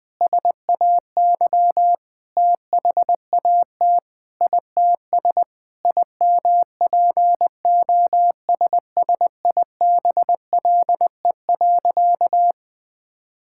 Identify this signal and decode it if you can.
Morse code, heard as a keyed tone, about 20 wpm, 700 Hz. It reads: SAY THAT ITS IMPOSSIBLE.
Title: Sentences with 6-Word Phrases - 20wpm